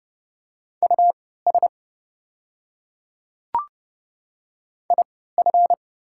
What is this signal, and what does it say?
Morse code, heard as a keyed tone, about 29 wpm, spaced as for 17 wpm, 700 Hz. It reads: US IF
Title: 3x Character Spacing - Top 200 Words - 30wpm